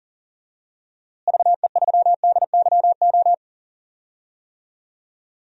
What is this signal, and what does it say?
Morse code, heard as a keyed tone, about 40 wpm, 700 Hz. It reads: VE3DYO
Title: DXCC-100 Call Signs - 6 Characters - Repeated 3x - 40wpm